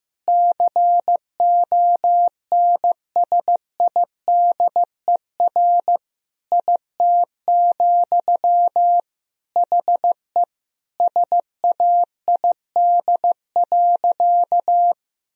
Morse code, heard as a keyed tone, about 15 wpm, 700 Hz. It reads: CONSIDER IT, HE SAID.
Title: Sentences from Top 1000 Words - Review Part 4 - 15wpm